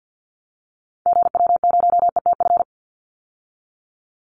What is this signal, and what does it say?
Morse code, heard as a keyed tone, about 50 wpm, 700 Hz. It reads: GW0ETF
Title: Call Signs - 50wpm